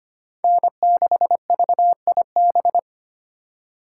Morse code, heard as a keyed tone, about 25 wpm, 700 Hz. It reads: N6VIB
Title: USA Call Signs - 5 Characters - 3x Character Spacing - 25wpm